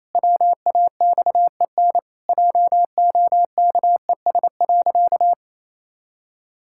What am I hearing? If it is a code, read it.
Morse code, heard as a keyed tone, about 28 wpm, 700 Hz. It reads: WAXEN JOKES.
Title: Pangram Challenge v2 - 28wpm